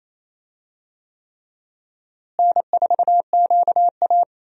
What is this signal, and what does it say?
Morse code, heard as a keyed tone, about 28 wpm, 700 Hz. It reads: N4QA
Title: USA Call Signs - 4 Characters - Repeated 3x - 28wpm